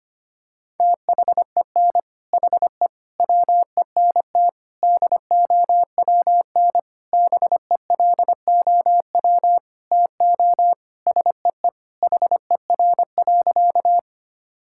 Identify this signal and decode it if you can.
Morse code, heard as a keyed tone, about 25 wpm, 700 Hz. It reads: THEN HE WENT DOWN BELOW TO SEE HER.
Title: Sentences from Top 1000 Words - Review Part 5 - 25wpm